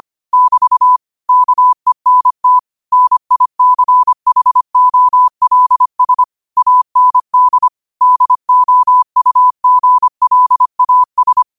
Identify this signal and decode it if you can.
Morse code, heard as a keyed tone, about 25 wpm, 1000 Hz. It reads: <BT> KENT NICHOLS AND DOUGLAS
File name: qotdfiles / 25wpm